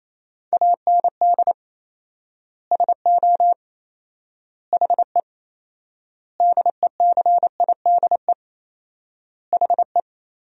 Morse code, heard as a keyed tone, about 27 wpm, 700 Hz. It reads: AND SO HE DECIDE HE
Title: 4x Word-Spacing - Sentences from Top 500 Words - 28wpm